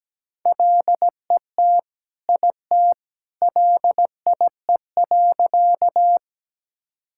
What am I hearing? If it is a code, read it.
Morse code, heard as a keyed tone, about 17 wpm, 700 Hz. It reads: LET IT LIE.
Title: Sentences from Top 1600 Words 17wpm